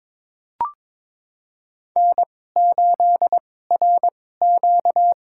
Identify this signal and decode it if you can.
Morse code, heard as a keyed tone, about 22 wpm, spaced as for 16 wpm, 700 Hz. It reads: N8RQ
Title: USA Call Signs - 2x Character Spacing - 22wpm